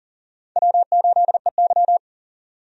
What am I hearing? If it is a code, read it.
Morse code, heard as a keyed tone, about 40 wpm, 700 Hz. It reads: W8EY
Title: USA Call Signs - 4 Characters - 40wpm